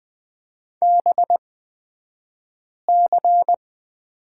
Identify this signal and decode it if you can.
Morse code, heard as a keyed tone, about 20 wpm, spaced as for 5 wpm, 700 Hz. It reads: BC
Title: 242 Lesson 39d - Letter z - Two Characters 20wpm